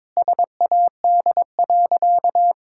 Morse code, heard as a keyed tone, about 22 wpm, 700 Hz. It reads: SAD.
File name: practice / 22wpm / Sentences with 2-Word Phrases - 22wpm